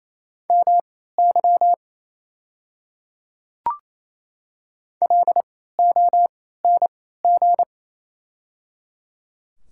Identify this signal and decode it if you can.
Morse code, heard as a keyed tone, about 28 wpm, spaced as for 16 wpm, 700 Hz. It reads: MY LONG
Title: 3x Character Spacing - Top 200 Words - 28wpm